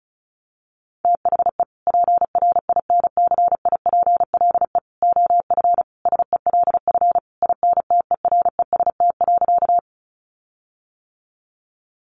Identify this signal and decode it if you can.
Morse code, heard as a keyed tone, about 34 wpm, 700 Hz. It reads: THE PRINCIPLE OF SELF INTEREST.
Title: Sentences from Top 1400 Words 35wpm